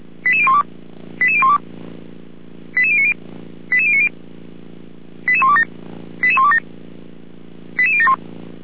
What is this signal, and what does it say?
Five-tone selective call (ZVEI), heard as quick runs of five tones, tones of about 70 ms. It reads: 88012 88012 88090 88090 88127 88127 88071